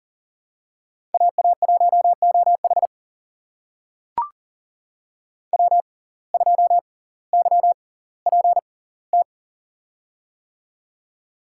Morse code, heard as a keyed tone, about 40 wpm, spaced as for 20 wpm, 700 Hz. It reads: AA1OH W2YPT
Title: USA Call Signs - 5 Characters - 6x Character Spacing - 40wpm